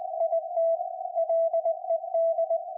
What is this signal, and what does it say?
Morse code, heard as a keyed tone, about 20 wpm, 660 Hz, with radio noise. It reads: IT LED
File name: IC-7100